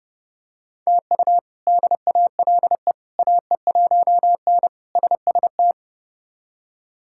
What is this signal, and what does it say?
Morse code, heard as a keyed tone, about 30 wpm, 700 Hz. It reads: TU DALE AE1N SST
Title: K1USN SST QSO Practice - 30wpm